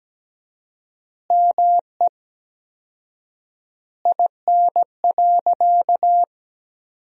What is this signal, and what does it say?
Morse code, heard as a keyed tone, about 17 wpm, 700 Hz. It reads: ME IN.
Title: 4x Word-Spacing - Sentences from Top 500 Words - 17wpm